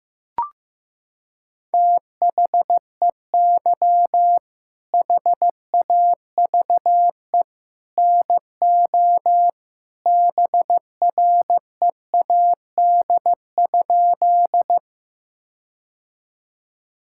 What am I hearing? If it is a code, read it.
Morse code, heard as a keyed tone, about 15 wpm, 700 Hz. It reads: THEY HAVE NO BREAD?
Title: Sentences from Top 1600 Words 15wpm